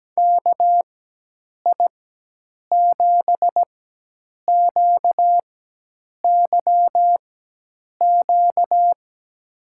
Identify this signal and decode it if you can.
Morse code, heard as a keyed tone, about 17 wpm, spaced as for 8 wpm, 700 Hz. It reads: KI7QYQ
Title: USA Call Signs - 6 Characters - 4x Character Spacing - 17wpm